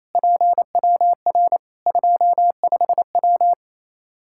Morse code, heard as a keyed tone, about 28 wpm, 700 Hz. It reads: PWR 25W
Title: Practice QSO - Exchange 2 - Info - 28wpm